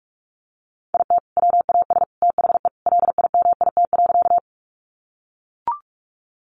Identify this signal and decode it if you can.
Morse code, heard as a keyed tone, about 44 wpm, 700 Hz. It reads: IT WAS THE LIMIT.
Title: Sentences from Top 1500 Words 45wpm